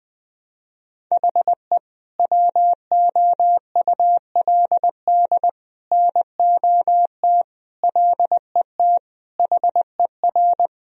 Morse code, heard as a keyed tone, about 21 wpm, 700 Hz. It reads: HE WOULD NOT LET HER
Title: Sentences from Top 1000 Words - Review Part 2 - 20wpm